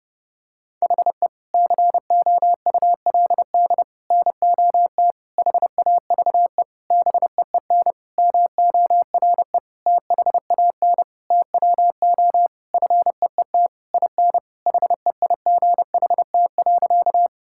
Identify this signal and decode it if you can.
Morse code, heard as a keyed tone, about 30 wpm, 700 Hz. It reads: HE COULD NOT HAVE BEEN MORE THAN TWO FEET IN HEIGHT.
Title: Sentences from Top 1600 Words 30wpm